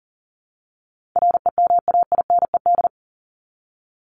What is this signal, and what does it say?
Morse code, heard as a keyed tone, about 40 wpm, 700 Hz. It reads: REMAINED
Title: ED Suffix - Top 100 Words with Base Words 40wpm